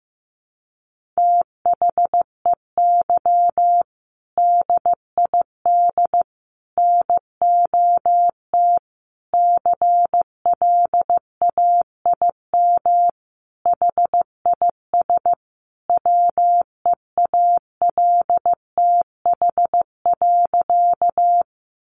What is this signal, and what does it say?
Morse code, heard as a keyed tone, about 15 wpm, 700 Hz. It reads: THEY DID NOT CLAIM HIS WEALTH.
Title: Sentences from Top 1500 Words 15wpm